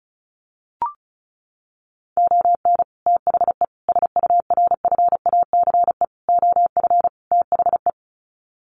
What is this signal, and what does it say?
Morse code, heard as a keyed tone, about 34 wpm, 700 Hz. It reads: ON THE SURFACE OF THE
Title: Sets of 5 Words Encore 35wpm